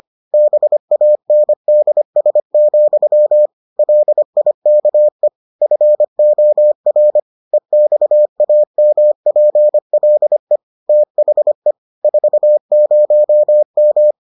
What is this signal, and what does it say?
Morse code, heard as a keyed tone, about 25 wpm, 600 Hz. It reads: BANDS, LIKE FOR EXAMPLE THE 40M